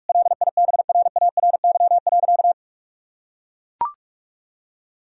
Morse code, heard as a keyed tone, about 43 wpm, 700 Hz. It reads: LIBRARY.
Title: Sentences from Top 1000 Words - Review Part 4 - 45wpm